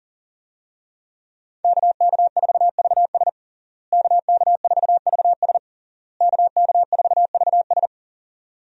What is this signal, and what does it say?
Morse code, heard as a keyed tone, about 40 wpm, 700 Hz. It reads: KK4VS KK4VS KK4VS
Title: USA Call Signs - 5 Characters - Repeated 3x - 40wpm